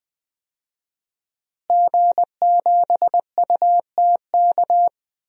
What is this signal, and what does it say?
Morse code, heard as a keyed tone, about 20 wpm, 700 Hz. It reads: G7UTK
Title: DXCC-100 Call Signs - 5 Characters - Repeated 2x - 20wpm